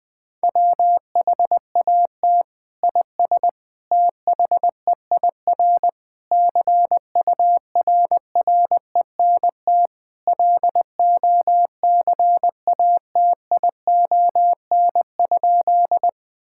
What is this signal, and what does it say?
Morse code, heard as a keyed tone, about 20 wpm, 700 Hz. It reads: WHAT IS THEIR CURRENT LOCATION?
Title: Sentences from Top 1000 Words - Review Part 6 - 20wpm